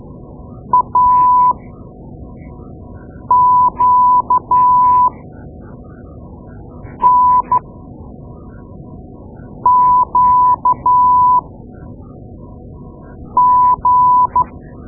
Morse code, heard as a keyed tone, about 12 wpm, spaced as for 4 wpm, 1000 Hz. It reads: AQNQG
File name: Morse Lessons / Morse Lessons / Tape-2A